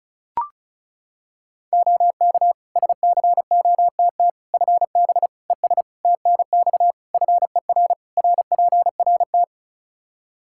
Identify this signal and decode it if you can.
Morse code, heard as a keyed tone, about 36 wpm, 700 Hz. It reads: OK SCOTT FB ES TNX FER RPRT
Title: Practice QSO - Exchange 2 - Info - 35wpm